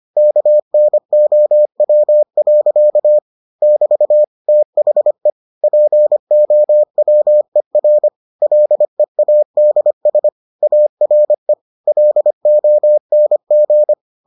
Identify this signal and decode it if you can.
Morse code, heard as a keyed tone, about 25 wpm, 600 Hz. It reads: KNOW. <BT> THE POWER LEADS ARE LONG